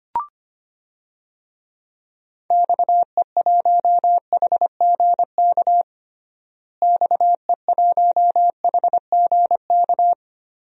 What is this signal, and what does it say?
Morse code, heard as a keyed tone, about 25 wpm, 700 Hz. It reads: XE1HGK XE1HGK
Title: DXCC-100 Call Signs - 6 Characters - Repeated 3x - 25wpm